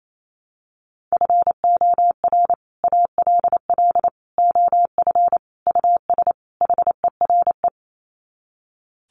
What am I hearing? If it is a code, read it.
Morse code, heard as a keyed tone, about 28 wpm, 700 Hz. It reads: FOR ALL OF US HERE
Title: Sets of 5 Words 28wpm